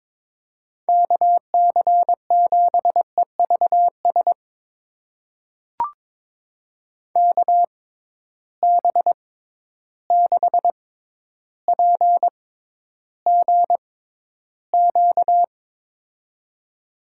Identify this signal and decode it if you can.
Morse code, heard as a keyed tone, about 22 wpm, spaced as for 11 wpm, 700 Hz. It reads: KC7EVS KB6PGQ
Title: USA Call Signs - 6 Characters - 6x Character Spacing - 22wpm